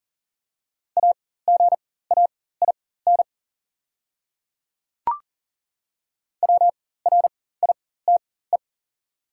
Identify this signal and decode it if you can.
Morse code, heard as a keyed tone, about 40 wpm, spaced as for 19 wpm, 700 Hz. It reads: AGAIN WRITE
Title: 4x Character Spacing - Top 500 Words - 40wpm